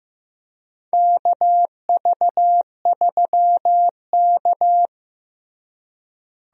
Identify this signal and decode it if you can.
Morse code, heard as a keyed tone, about 15 wpm, 700 Hz. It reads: KV3K
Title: USA Call Signs - 4 Characters - Repeated 2x - 15wpm